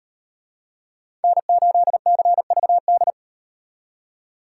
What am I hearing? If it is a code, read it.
Morse code, heard as a keyed tone, about 36 wpm, 700 Hz. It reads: N8CVD